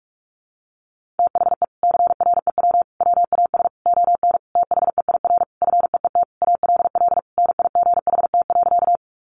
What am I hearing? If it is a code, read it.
Morse code, heard as a keyed tone, about 44 wpm, 700 Hz. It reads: THE CREW WAS ON THEIR FEET ALL NIGHT.
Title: Sentences from Top 1500 Words 45wpm